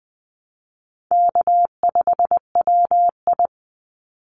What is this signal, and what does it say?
Morse code, heard as a keyed tone, about 20 wpm, 690 Hz. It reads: K5WI
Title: NA SOTA Chaser Callsigns 20wpm